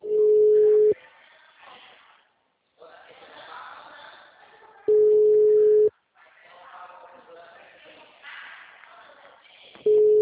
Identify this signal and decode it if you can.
Ringback tone (Europe, 425 Hz), with speech.
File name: out- -2002- - - .67690